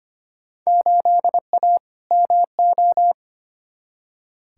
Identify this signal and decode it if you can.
Morse code, heard as a keyed tone, about 25 wpm, 700 Hz. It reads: 8A MO
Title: Field Day Exchanges Encore - 25wpm